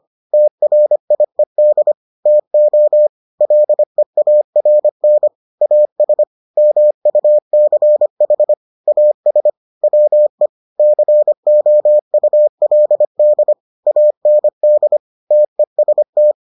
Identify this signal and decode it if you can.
Morse code, heard as a keyed tone, about 25 wpm, 600 Hz. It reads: TRIED TO LEARN AS MUCH AS WE COULD AND TEST